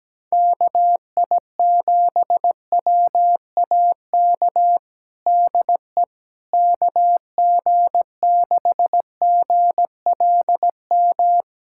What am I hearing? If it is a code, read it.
Morse code, heard as a keyed tone, about 17 wpm, 700 Hz. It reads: KI7WAK DE KG6GLM
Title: USA Callsign DE Callsign - 6 Characters - 17wpm